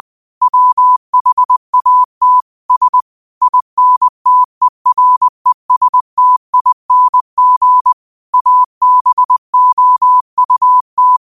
Morse code, heard as a keyed tone, about 20 wpm, 1000 Hz. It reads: WHAT S INTERESTING ABOUT